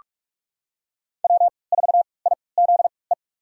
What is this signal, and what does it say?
Morse code, heard as a keyed tone, about 46 wpm, spaced as for 26 wpm, 700 Hz. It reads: W4IZE
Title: USA Call Signs - 3x Character Spacing - 45wpm